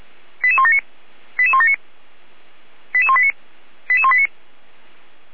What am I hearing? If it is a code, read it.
Five-tone selective call (ZVEI), heard as quick runs of five tones, tones of about 70 ms. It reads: 88179 88179 88189 88189
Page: Server at audio.f85360.feuerwehr-orlamuende.org